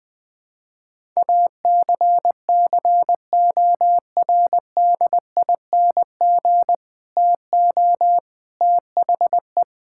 Morse code, heard as a keyed tone, about 20 wpm, 700 Hz. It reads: ACCORDING TO THE
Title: Build A Phrase - 3-Word Phrases - 20wpm